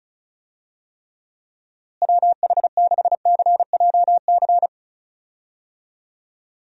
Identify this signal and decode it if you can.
Morse code, heard as a keyed tone, about 36 wpm, 700 Hz. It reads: WH6CJC
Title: USA Call Signs - 6 Characters - Repeated 3x - 35wpm